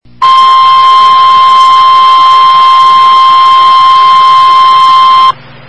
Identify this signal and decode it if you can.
Emergency Alert System (SAME) tones (the 1050 Hz attention tone), with noise. It is heard from 0.2 to 5.4 s.
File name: RR Skywarn / WX5OKC Amateur Repeater Hz